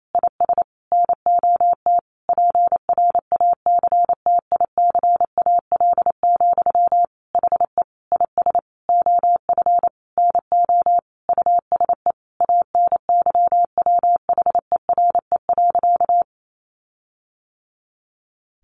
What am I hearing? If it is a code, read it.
Morse code, heard as a keyed tone, about 28 wpm, 700 Hz. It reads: IS NOT PRACTICAL, HE IS OF NO USE ANYWHERE.
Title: Sentences from Top 1400 Words 28wpm